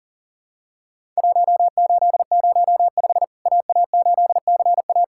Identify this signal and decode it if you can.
Morse code, heard as a keyed tone, about 40 wpm, 700 Hz. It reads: 1805 AA8CA
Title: Worked All Europe DX Contest - QTC Exchange - 40wpm